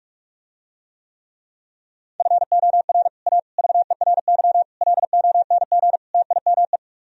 Morse code, heard as a keyed tone, about 44 wpm, 700 Hz. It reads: FOR A VERY LONG TIME